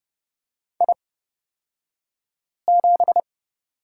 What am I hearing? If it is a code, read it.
Morse code, heard as a keyed tone, about 30 wpm, spaced as for 5 wpm, 700 Hz. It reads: I7
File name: Lesson 37d - Letter x - Two Characters 30wpm